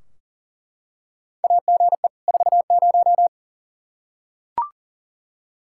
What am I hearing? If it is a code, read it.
Morse code, heard as a keyed tone, about 40 wpm, 700 Hz. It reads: AGE 40
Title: Practice QSO - Exchange 3 - More Info - 40wpm